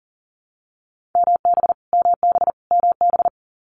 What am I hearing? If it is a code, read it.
Morse code, heard as a keyed tone, about 40 wpm, 700 Hz. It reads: MB MB MB
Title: Canadian Provinces and Territories - Repeated 3x - 40wpm